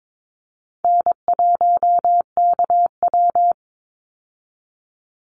Morse code, heard as a keyed tone, about 22 wpm, 700 Hz. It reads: N1KW
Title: Easy Call Signs - 22wpm